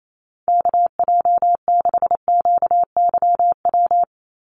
Morse code, heard as a keyed tone, about 28 wpm, 700 Hz. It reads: KJ6QYW
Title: USA Call Signs 28wpm